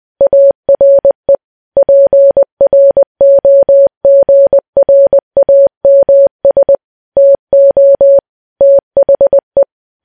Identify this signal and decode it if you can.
Morse code, heard as a keyed tone, about 20 wpm, 560 Hz. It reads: ARE PROGRAMS TO THE
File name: morse